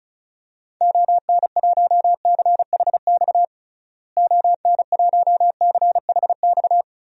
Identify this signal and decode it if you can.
Morse code, heard as a keyed tone, about 34 wpm, 700 Hz. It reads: ON1CHX ON1CHX
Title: DXCC-100 Call Signs - 6 Characters - Repeated 2x - 35wpm